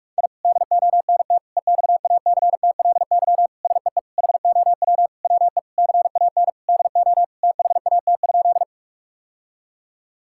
Morse code, heard as a keyed tone, about 46 wpm, 700 Hz. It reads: I DONT EXACTLY SEE HOW WE CAN DO THAT?
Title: Sentences from Top 1000 Words - Review Part 6 - 45wpm